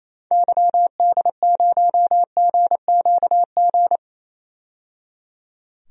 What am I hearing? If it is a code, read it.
Morse code, heard as a keyed tone, about 28 wpm, 700 Hz. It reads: YD0GQG
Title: DXCC-100 Call Signs - 6 Characters - 28wpm